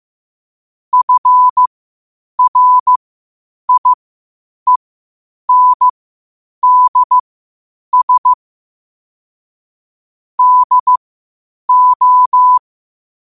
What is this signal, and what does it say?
Morse code, heard as a keyed tone, about 15 wpm, spaced as for 8 wpm, 1000 Hz. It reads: FRIENDS DO
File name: qotdfiles / 8wpm